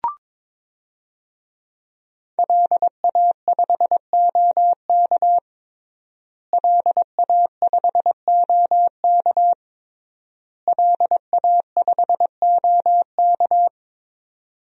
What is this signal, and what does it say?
Morse code, heard as a keyed tone, about 22 wpm, 700 Hz. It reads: LA5OK LA5OK LA5OK
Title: DXCC-100 Call Signs - 5 Characters - Repeated 3x - 22wpm